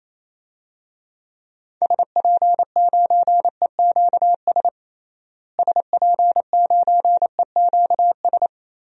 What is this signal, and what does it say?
Morse code, heard as a keyed tone, about 28 wpm, 700 Hz. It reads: SP9EQS SP9EQS
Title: DXCC-100 Call Signs - 6 Characters - Repeated 3x - 28wpm